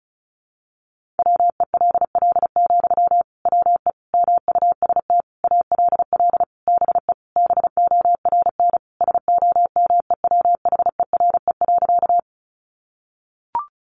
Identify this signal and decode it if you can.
Morse code, heard as a keyed tone, about 34 wpm, 700 Hz. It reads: WELL, WE MUST ALL BE BORN SOMEWHERE.
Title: Sentences from Top 1400 Words 35wpm